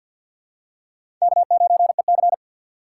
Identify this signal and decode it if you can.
Morse code, heard as a keyed tone, about 46 wpm, 700 Hz. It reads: K9EC
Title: USA Call Signs - 3x Character Spacing - 50wpm